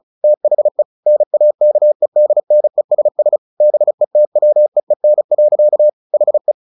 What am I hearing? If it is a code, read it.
Morse code, heard as a keyed tone, about 34 wpm, 600 Hz. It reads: THE NAKEDNESS BETWEEN. HE
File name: the old man and the sea 2 35wpm